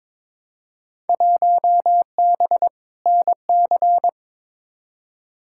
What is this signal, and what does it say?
Morse code, heard as a keyed tone, about 22 wpm, 700 Hz. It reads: 1B NC
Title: Field Day Exchanges Encore - 22wpm